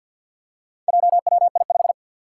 Morse code, heard as a keyed tone, about 52 wpm, 700 Hz. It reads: JWI5
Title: Four Challenging Letter-Number Combinations 50wpm